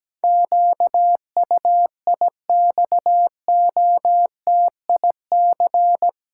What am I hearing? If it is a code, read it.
Morse code, heard as a keyed tone, about 17 wpm, 700 Hz. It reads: QUIXOTIC
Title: Pangram Challenge v2 - 17wpm